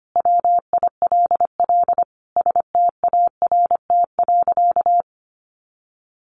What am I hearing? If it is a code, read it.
Morse code, heard as a keyed tone, about 25 wpm, 700 Hz. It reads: WILL START.
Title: Sentences from Top 300 Words 25wpm